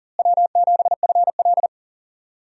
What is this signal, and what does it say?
Morse code, heard as a keyed tone, about 40 wpm, 700 Hz. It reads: W7FL
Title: USA Call Signs - 4 Characters - Repeated 2x - 40wpm